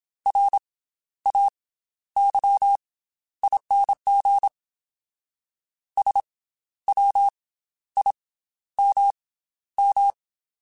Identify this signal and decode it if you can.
Morse code, heard as a keyed tone, about 27 wpm, 800 Hz. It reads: R A Y ING S W I M M
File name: Special INGs / Special INGs-mp3s / Slow ing @25